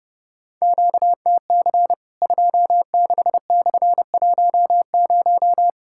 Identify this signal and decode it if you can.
Morse code, heard as a keyed tone, about 32 wpm, 700 Hz. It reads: QTC 26/10
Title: Worked All Europe DX Contest - QTC Exchange - 30wpm